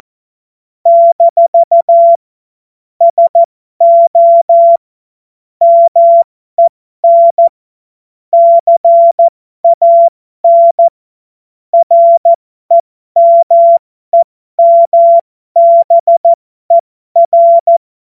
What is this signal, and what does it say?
Morse code, heard as a keyed tone, about 14 wpm, spaced as for 12 wpm, 680 Hz. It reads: - SO MEN CAN REMEMBER
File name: Morsekurs / Witze